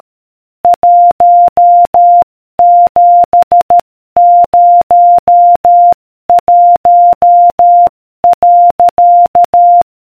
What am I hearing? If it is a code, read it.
Morse code, heard as a keyed tone, about 13 wpm, 700 Hz. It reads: 1701.